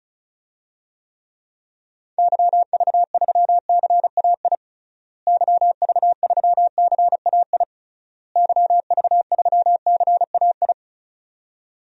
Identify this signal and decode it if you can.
Morse code, heard as a keyed tone, about 34 wpm, 700 Hz. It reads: YV3CAI YV3CAI YV3CAI
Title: DXCC-100 Call Signs - 6 Characters - Repeated 3x - 35wpm